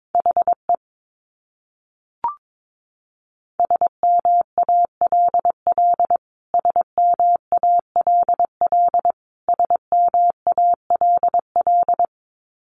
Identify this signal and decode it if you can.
Morse code, heard as a keyed tone, about 22 wpm, 700 Hz. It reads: HE SMALL SMALL SMALL
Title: Top 300 Words Repeated 22wpm